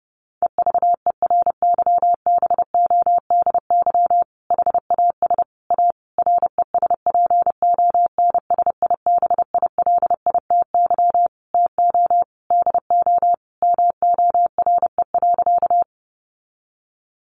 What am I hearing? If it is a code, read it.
Morse code, heard as a keyed tone, about 30 wpm, 700 Hz. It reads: EVERYBODY HAS A RESPONSIBILITY TO DO MORE.
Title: Sentences from Top 1400 Words 30wpm